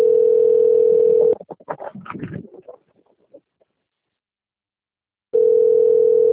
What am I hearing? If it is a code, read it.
Ringback tone (North America), 440 + 480 Hz.
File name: out- - - - - .2421